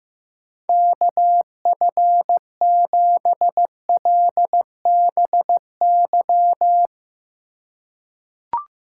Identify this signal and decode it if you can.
Morse code, heard as a keyed tone, about 15 wpm, 700 Hz. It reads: KF7LBY
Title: USA Call Signs - 4x Character Spacing - 15wpm